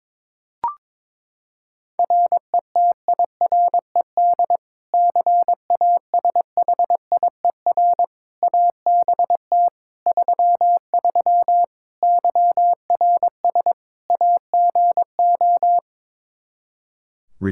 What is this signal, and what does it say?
Morse code, heard as a keyed tone, about 22 wpm, 700 Hz. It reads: RETIRED CASHIER ABT 33 YRS AGO
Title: Practice QSO - Exchange 3 - More Info - 22wpm